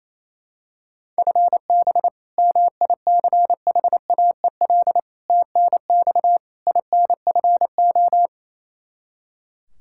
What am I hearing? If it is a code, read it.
Morse code, heard as a keyed tone, about 28 wpm, 700 Hz. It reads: FB MICHAEL TNX INFO
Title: Practice QSO - Exchange 3 - More Info - 28wpm